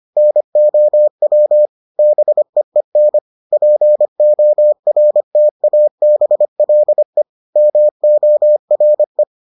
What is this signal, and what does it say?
Morse code, heard as a keyed tone, about 25 wpm, 600 Hz. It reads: NOW BEEN PORTABLE MORE